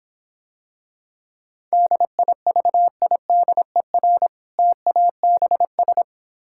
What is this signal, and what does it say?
Morse code, heard as a keyed tone, about 26 wpm, 700 Hz. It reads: DIVIDER TABS